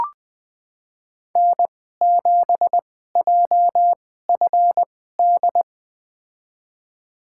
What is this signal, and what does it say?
Morse code, heard as a keyed tone, about 20 wpm, spaced as for 14 wpm, 700 Hz. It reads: N7JFD
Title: USA Call Signs - 5 Characters - 2x Character Spacing - 20wpm